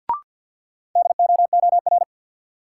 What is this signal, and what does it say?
Morse code, heard as a keyed tone, about 53 wpm, 700 Hz. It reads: DOOR
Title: Sets of 2 Words - New Word Drill - 50wpm